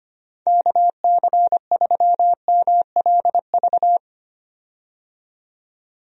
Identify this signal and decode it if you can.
Morse code, heard as a keyed tone, about 25 wpm, 700 Hz. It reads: KC3MLV
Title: USA Call Signs - 6 Characters - 25wpm